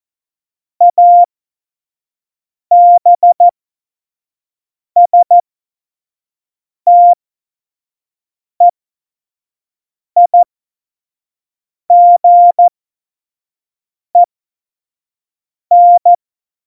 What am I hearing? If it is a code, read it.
Morse code, heard as a keyed tone, about 14 wpm, spaced as for 5 wpm, 700 Hz. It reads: ABSTEIGEN
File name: Morsekurs / Witze / Witz 26. Februar 2019, 25 F70